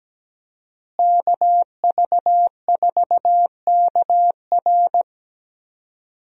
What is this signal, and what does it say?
Morse code, heard as a keyed tone, about 17 wpm, 700 Hz. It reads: KV4KR
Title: USA Call Signs - 5 Characters - Repeated 3x - 17wpm